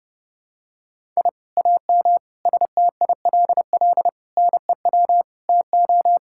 Morse code, heard as a keyed tone, about 32 wpm, 700 Hz. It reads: I AM STILL NEW TO
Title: Sentences from Top 1000 Words - Review Part 6 - 30wpm